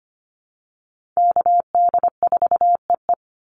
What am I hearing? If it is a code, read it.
Morse code, heard as a keyed tone, about 25 wpm, 700 Hz. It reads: KD4EE
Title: Call Signs - 25wpm